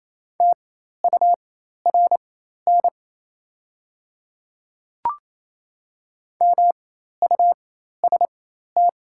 Morse code, heard as a keyed tone, about 27 wpm, spaced as for 13 wpm, 700 Hz. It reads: TURN MUST
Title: 4x Character Spacing - Top 200 Words - 28wpm